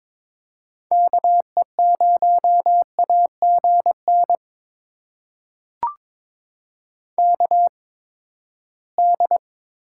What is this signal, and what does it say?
Morse code, heard as a keyed tone, about 22 wpm, spaced as for 15 wpm, 700 Hz. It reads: KE0AGN KD
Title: USA Call Signs - 6 Characters - 8x Character Spacing - 22wpm